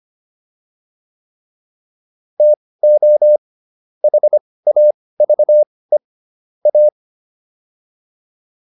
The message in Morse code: TO HAVE A

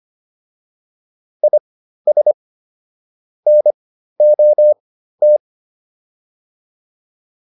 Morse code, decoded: IS NOT

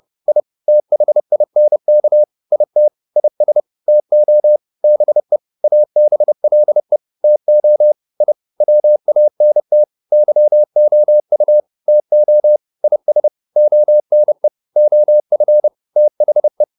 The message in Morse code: I THINK IT IS TO BE ABLE TO I WANT YOU TO IS ONE OF THE

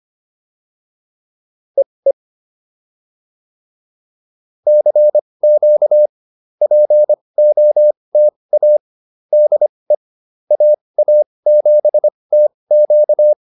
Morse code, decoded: EE CQ POTA DE AA7TQ

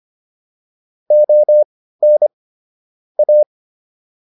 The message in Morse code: ON A

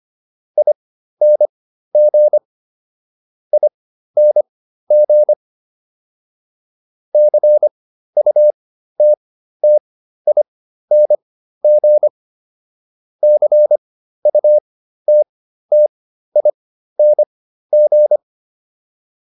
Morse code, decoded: ING ING CUTTING CUTTING

